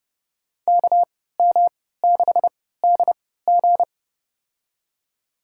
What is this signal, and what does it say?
Morse code, heard as a keyed tone, about 32 wpm, spaced as for 17 wpm, 700 Hz. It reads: KM6DG
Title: USA Call Signs - 5 Characters - 3x Character Spacing - 30wpm